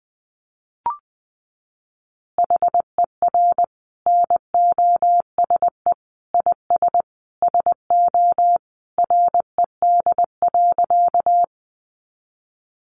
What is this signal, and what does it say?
Morse code, heard as a keyed tone, about 20 wpm, 700 Hz. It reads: HER NOSE IS SO RED.
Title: Easy Phrases 20wpm